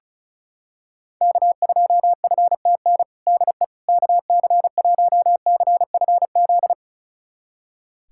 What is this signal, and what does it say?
Morse code, heard as a keyed tone, about 36 wpm, 700 Hz. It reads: K2FTN DE KC1CFZ
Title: USA Callsign DE Callsign 35wpm